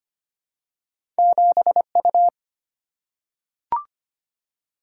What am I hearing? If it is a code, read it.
Morse code, heard as a keyed tone, about 25 wpm, 700 Hz. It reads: 7U
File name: Two Letter-Number Combinations - 25wpm